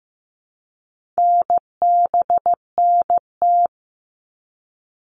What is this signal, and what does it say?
Morse code, heard as a keyed tone, about 15 wpm, 700 Hz. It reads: NBNT
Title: All Cut Numbers - 4-Digit 15wpm